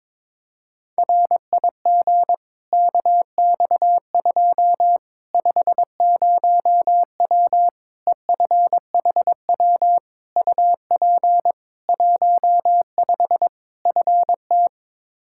Morse code, heard as a keyed tone, about 22 wpm, 700 Hz. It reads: RIG KX2 50W EFHW UP 15 FT
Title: K1USN SST QSO Practice 22wpm